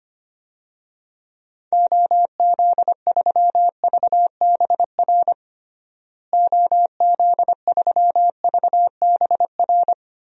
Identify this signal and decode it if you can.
Morse code, heard as a keyed tone, about 25 wpm, 700 Hz. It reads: OZ3VBR OZ3VBR
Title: DXCC-100 Call Signs - 6 Characters - Repeated 2x - 25wpm